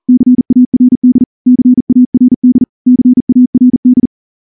Morse code, heard as a keyed tone, about 38 wpm, 250 Hz. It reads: CARD CARD CARD
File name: morse-training-40WPM